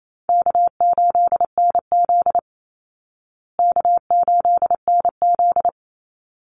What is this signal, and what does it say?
Morse code, heard as a keyed tone, about 28 wpm, 700 Hz. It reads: K8NZ K8NZ
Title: Easy Call Signs Repeated 2x - 28wpm